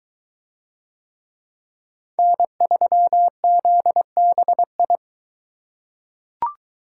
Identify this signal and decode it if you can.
Morse code, heard as a keyed tone, about 21 wpm, 700 Hz. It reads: N3ZBI